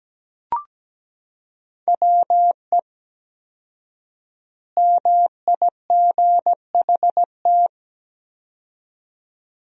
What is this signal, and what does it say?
Morse code, heard as a keyed tone, about 17 wpm, 700 Hz. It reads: WE MIGHT
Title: 4x Word-Spacing - Sentences from Top 300 Words - 17wpm